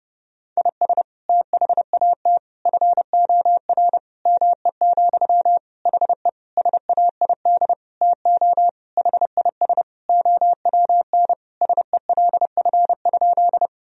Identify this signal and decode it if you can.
Morse code, heard as a keyed tone, about 30 wpm, 700 Hz. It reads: IS THAT FOR ME, HE SAID TO HIS OWN SELF?
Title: Sentences from Top 1000 Words - Review Part 1 - 30wpm